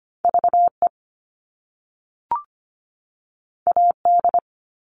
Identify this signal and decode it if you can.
Morse code, heard as a keyed tone, about 25 wpm, 700 Hz. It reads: V E A D
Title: All Cut Numbers - 2-Digit 25wpm